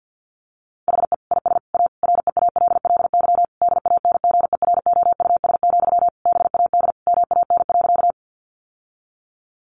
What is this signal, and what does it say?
Morse code, heard as a keyed tone, about 50 wpm, 700 Hz. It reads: HE IS A REALLY DANGEROUS, BAD MAN.
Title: Sentences from Top 1200 Words 50wpm